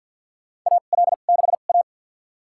Morse code, heard as a keyed tone, about 52 wpm, spaced as for 36 wpm, 700 Hz. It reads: AL6A
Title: USA Call Signs - 4 Characters - 2x Character Spacing - 50wpm